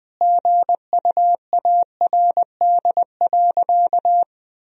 Morse code, heard as a keyed tone, about 20 wpm, 700 Hz. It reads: GUARD.
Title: Sentences from Top 1000 Words - Review Part 4 - 20wpm